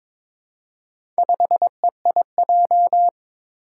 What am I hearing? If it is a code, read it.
Morse code, heard as a keyed tone, about 23 wpm, 700 Hz. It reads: 5EIJ
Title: Four Challenging Letter-Number Combinations 22wpm